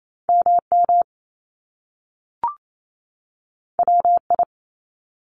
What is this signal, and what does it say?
Morse code, heard as a keyed tone, about 28 wpm, 700 Hz. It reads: MM WI
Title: Top Two Letter Combinations in Words 28wpm